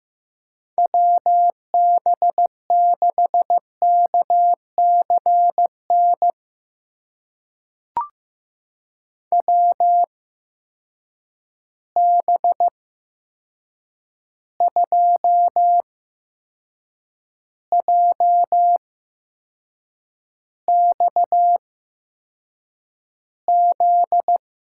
Morse code, heard as a keyed tone, about 15 wpm, spaced as for 6 wpm, 700 Hz. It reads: WB6KCN WB2JXZ
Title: USA Call Signs - 6 Characters - 8x Character Spacing - 15wpm